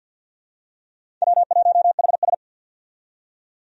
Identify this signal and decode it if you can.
Morse code, heard as a keyed tone, about 52 wpm, 700 Hz. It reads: W1HS
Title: USA Call Signs - 4 Characters - 50wpm